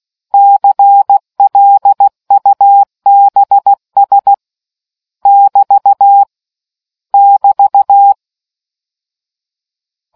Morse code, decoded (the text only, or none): CLUBS <BT> <BT>